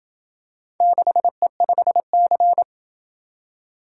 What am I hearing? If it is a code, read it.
Morse code, heard as a keyed tone, about 26 wpm, 700 Hz. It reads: 6E5C